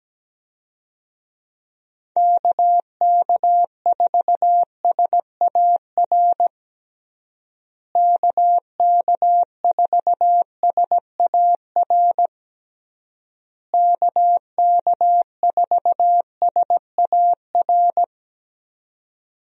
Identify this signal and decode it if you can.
Morse code, heard as a keyed tone, about 17 wpm, 700 Hz. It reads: KK4SAR KK4SAR KK4SAR
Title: USA Call Signs - 6 Characters - Repeated 3x - 17wpm